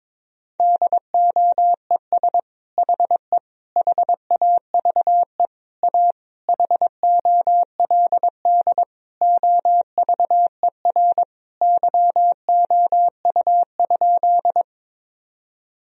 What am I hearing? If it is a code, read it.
Morse code, heard as a keyed tone, about 22 wpm, 700 Hz. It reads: DOES HE HAVE A HOLD OVER YOU?
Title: Sentences from Top 1000 Words - Review Part 2 - 22wpm